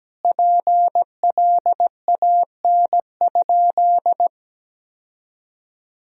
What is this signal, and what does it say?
Morse code, heard as a keyed tone, about 17 wpm, 700 Hz. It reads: PLAN?
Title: Sentences from Top 1000 Words - Review Part 4 - 17wpm